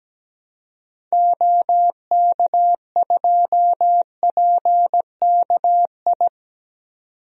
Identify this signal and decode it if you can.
Morse code, heard as a keyed tone, about 17 wpm, 700 Hz. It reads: OK2PKI
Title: DXCC-100 Call Signs - 6 Characters - 17wpm